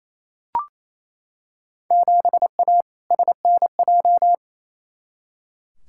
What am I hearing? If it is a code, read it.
Morse code, heard as a keyed tone, about 28 wpm, 700 Hz. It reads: 7A SNJ
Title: Field Day Exchanges Encore - 28wpm